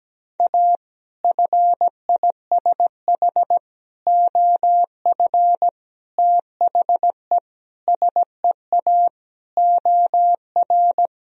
Morse code, heard as a keyed tone, about 17 wpm, 700 Hz. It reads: A FISH OF THE SEA OR
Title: Sentences from Top 1000 Words - Review Part 1 - 17wpm